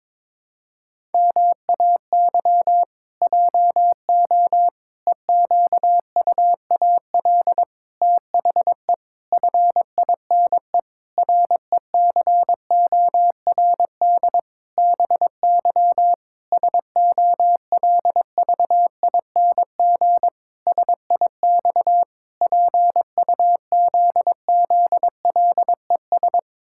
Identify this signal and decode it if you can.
Morse code, heard as a keyed tone, about 22 wpm, 700 Hz. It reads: MAY JO EQUAL THE FINE RECORD BY SOLVING SIX PUZZLES
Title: Pangram Challenge v2 - 22wpm